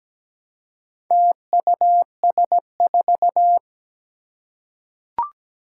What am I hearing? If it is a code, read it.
Morse code, heard as a keyed tone, about 17 wpm, 700 Hz. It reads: TUS4
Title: Four Challenging Letter-Number Combinations 17wpm